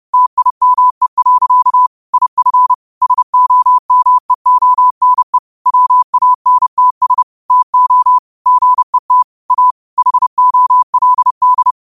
Morse code, heard as a keyed tone, about 30 wpm, 1000 Hz. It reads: TIME. IF SOMEONE WANTS TO GET A HOLD
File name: qotdfiles / 30wpm